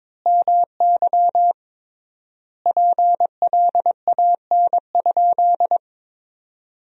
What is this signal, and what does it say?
Morse code, heard as a keyed tone, about 21 wpm, 700 Hz. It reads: MY PLAN?
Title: 3x Word-Spacing - Sentences from Top 500 Words - 22wpm